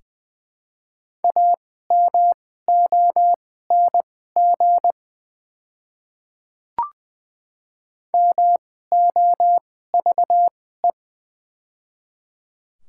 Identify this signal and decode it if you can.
Morse code, heard as a keyed tone, about 20 wpm, spaced as for 14 wpm, 700 Hz. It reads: AMONG MOVE
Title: 2x Character Spacing - Top 500 Words - 20wpm